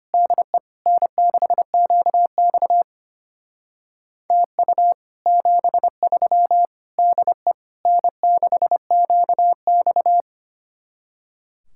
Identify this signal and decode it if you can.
Morse code, heard as a keyed tone, about 31 wpm, 700 Hz. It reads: DE N6QX TU 73 DE N6QX
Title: Speed-Racing Common QSO Phrases - 25wpm